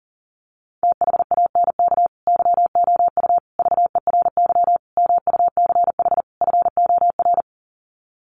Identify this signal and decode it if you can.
Morse code, heard as a keyed tone, about 40 wpm, 700 Hz. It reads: THANK YOU VERY MUCH FOR
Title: Sets of 5 Words 40wpm